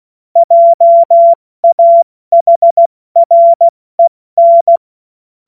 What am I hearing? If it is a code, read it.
Morse code, heard as a keyed tone, about 16 wpm, 680 Hz. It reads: JAHREN
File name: Morsekurs / Witze / Witz 19. April 2022 70 F80